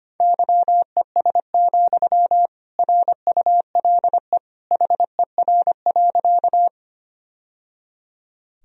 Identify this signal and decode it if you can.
Morse code, heard as a keyed tone, about 25 wpm, 700 Hz. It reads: YES, RULE HER.
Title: Sentences from Top 1000 Words - Review Part 2 - 25wpm